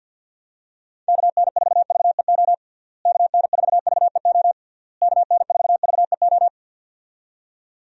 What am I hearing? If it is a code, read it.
Morse code, heard as a keyed tone, about 50 wpm, 700 Hz. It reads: KN4VEO KN4VEO KN4VEO
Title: USA Call Signs - 6 Characters - Repeated 3x - 50wpm